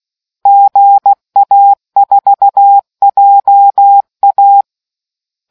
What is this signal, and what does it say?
Morse code, heard as a keyed tone, about 16 wpm, 800 Hz. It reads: GA4JA